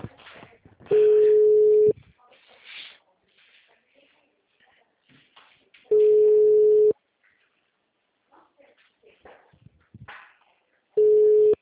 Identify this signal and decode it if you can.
Ringback tone (Europe, 425 Hz), with speech.
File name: out- -2017- - - .78494